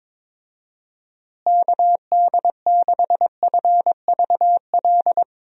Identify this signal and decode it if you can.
Morse code, heard as a keyed tone, about 22 wpm, 700 Hz. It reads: KD6FVL
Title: USA Call Signs - 6 Characters - Repeated 2x - 22wpm